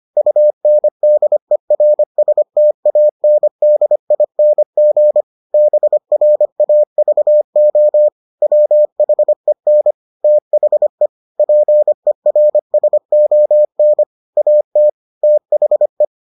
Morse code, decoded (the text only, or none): UNDERSTANDING BRAVO WHEN THE PERSON AT THE